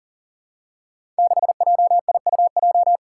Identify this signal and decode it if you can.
Morse code, heard as a keyed tone, about 41 wpm, 700 Hz. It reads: 6JIUJ
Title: Five Challenging Letter-Number Combinations 40wpm